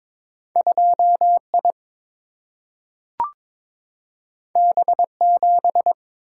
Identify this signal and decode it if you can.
Morse code, heard as a keyed tone, about 22 wpm, 690 Hz. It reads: 2I B7
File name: practice / 22wpm / Two Letter-Number Combinations - 22wpm